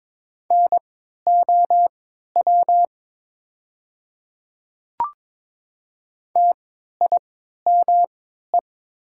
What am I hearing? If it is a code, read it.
Morse code, heard as a keyed tone, about 22 wpm, spaced as for 12 wpm, 700 Hz. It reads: NOW TIME
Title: 3x Character Spacing - Top 100 Words - 22wpm